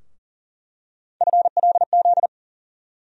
Morse code, heard as a keyed tone, about 40 wpm, 700 Hz. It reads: FLZ
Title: USA Call Sign Postfixes - 3 Letters 40wpm